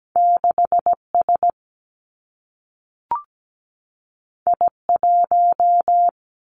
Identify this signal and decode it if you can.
Morse code, heard as a keyed tone, about 17 wpm, 700 Hz. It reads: 6S I1
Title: Two Challenging Letter-Number Combinations 17wpm